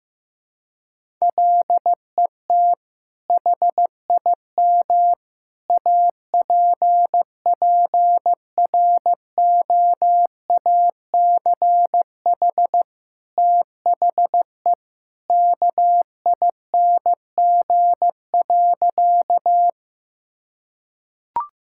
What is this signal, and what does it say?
Morse code, heard as a keyed tone, about 16 wpm, 700 Hz. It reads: LET HIM APPROACH THE KING.
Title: Sentences from Top 1000 Words - Review Part 6 - 15wpm